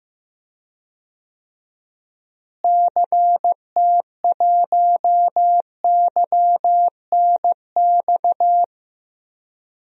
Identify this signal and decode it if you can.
Morse code, heard as a keyed tone, about 15 wpm, 700 Hz. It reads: CT1YNX
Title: DXCC-100 Call Signs - 6 Characters - Repeated 3x - 15wpm